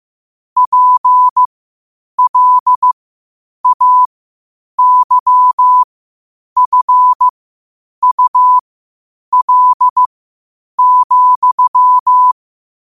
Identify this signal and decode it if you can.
Morse code, heard as a keyed tone, about 15 wpm, spaced as for 8 wpm, 1000 Hz. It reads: PLAYFUL,
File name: qotdfiles / 8wpm